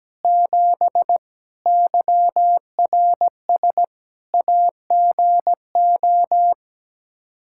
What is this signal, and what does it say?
Morse code, heard as a keyed tone, about 17 wpm, 700 Hz. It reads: 7 YRS AGO
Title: K1USN SST QSO Practice 17wpm